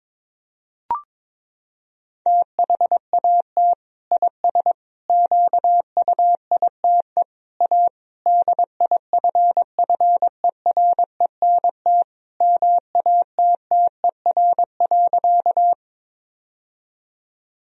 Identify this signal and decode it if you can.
Morse code, heard as a keyed tone, about 22 wpm, 700 Hz. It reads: THAT IS QUITE A DIFFERENT MATTER.
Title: Sentences from Top 1000 Words - Review Part 3 - 22wpm